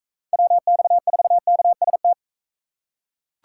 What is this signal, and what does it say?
Morse code, heard as a keyed tone, about 40 wpm, 700 Hz. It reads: WX4KST